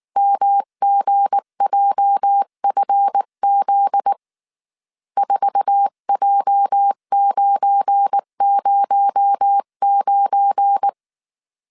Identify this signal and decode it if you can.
Morse code, heard as a keyed tone, about 19 wpm, 800 Hz. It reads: MGJFZ 4J909